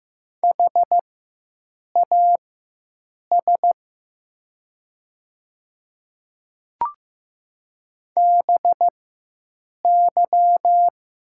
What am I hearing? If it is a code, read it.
Morse code, heard as a keyed tone, about 15 wpm, spaced as for 7 wpm, 700 Hz. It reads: HAS BY
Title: 4x Character Spacing - Top 100 Words - 15wpm